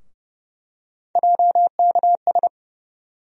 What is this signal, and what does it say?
Morse code, heard as a keyed tone, about 30 wpm, 700 Hz. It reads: JKS
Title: USA Call Sign Postfixes - 3 Letters 30wpm